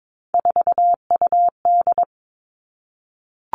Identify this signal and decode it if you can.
Morse code, heard as a keyed tone, about 22 wpm, 700 Hz. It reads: 4UD
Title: All Cut Numbers - 3-Digit 22wpm